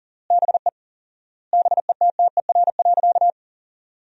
Morse code, heard as a keyed tone, about 40 wpm, 700 Hz. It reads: BE BETTER.
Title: 4x Word-Spacing - Sentences from Top 400 Words - 40wpm